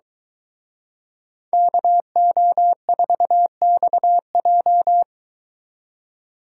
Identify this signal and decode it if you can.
Morse code, heard as a keyed tone, about 22 wpm, 700 Hz. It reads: KO4XJ